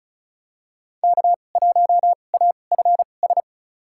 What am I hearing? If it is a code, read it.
Morse code, heard as a keyed tone, about 36 wpm, spaced as for 25 wpm, 700 Hz. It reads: K1AFS
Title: USA Call Signs - 5 Characters - 2x Character Spacing - 35wpm